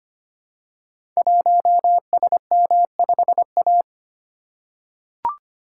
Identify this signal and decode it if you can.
Morse code, heard as a keyed tone, about 25 wpm, 700 Hz. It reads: 1SM5A
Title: Code Groups - 5 Letter:Numbers - 25wpm